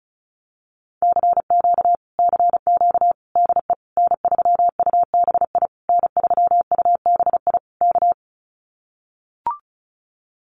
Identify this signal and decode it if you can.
Morse code, heard as a keyed tone, about 34 wpm, 700 Hz. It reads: CQ CQ DE N3UBI N3UBI K
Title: Common QSO Phrases Encore 35wpm